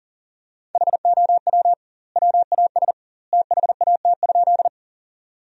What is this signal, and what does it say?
Morse code, heard as a keyed tone, about 40 wpm, 700 Hz. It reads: HOW WAS THAT?
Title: 2x Word-Spacing - Sentences from Top 100 Words - 40wpm